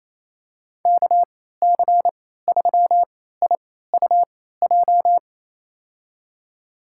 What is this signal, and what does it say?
Morse code, heard as a keyed tone, about 28 wpm, spaced as for 16 wpm, 700 Hz. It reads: KC3IUJ
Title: USA Call Signs - 6 Characters - 3x Character Spacing - 28wpm